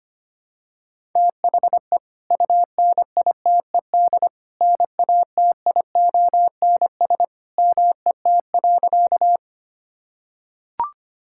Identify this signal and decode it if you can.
Morse code, heard as a keyed tone, about 25 wpm, 700 Hz. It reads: THE UNITED NATIONS MET.
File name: practice / 25wpm / Sentences with 2-Word Phrases - 25wpm